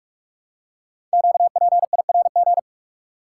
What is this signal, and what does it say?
Morse code, heard as a keyed tone, about 44 wpm, 700 Hz. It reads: QPIRG
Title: Code Groups - 5 Letters - 45wpm